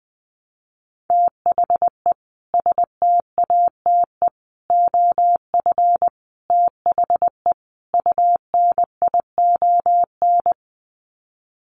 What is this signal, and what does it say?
Morse code, heard as a keyed tone, about 20 wpm, 700 Hz. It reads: THE STATE OF THE UNION
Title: Sets of 5 Words Encore 20wpm